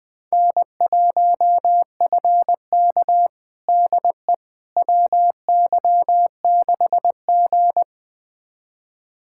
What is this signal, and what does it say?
Morse code, heard as a keyed tone, about 21 wpm, 700 Hz. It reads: N1FK DE WY6G
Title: USA Callsign DE Callsign - 4 Characters - 20wpm